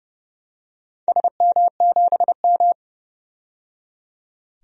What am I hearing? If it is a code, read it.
Morse code, heard as a keyed tone, about 31 wpm, 700 Hz. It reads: SM7M
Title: DXCC-100 Call Signs - 4 Characters - 30wpm